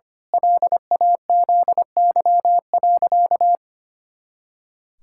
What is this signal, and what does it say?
Morse code, heard as a keyed tone, about 25 wpm, 700 Hz. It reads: LAZY.
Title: Pangram Challenge v2 - 25wpm